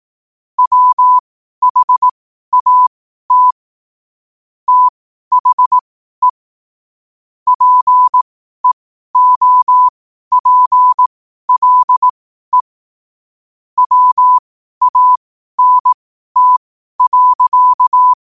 Morse code, heard as a keyed tone, about 18 wpm, spaced as for 12 wpm, 1000 Hz. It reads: WHAT THE PEOPLE WANT.
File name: qotdfiles / 12wpm